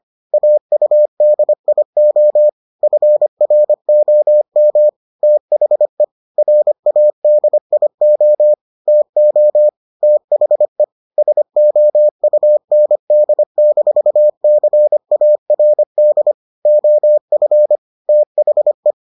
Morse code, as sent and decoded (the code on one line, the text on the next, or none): .- ..- -.. .. --- / ..-. .-. --- -- / - .... / .-. .- -.. .. --- / - --- / - .... / ... --- ..- -. -.. -....- -.-. .- .-. -.. / --- ..-. / - .... .
AUDIO FROM THE RADIO TO THE SOUND-CARD OF THE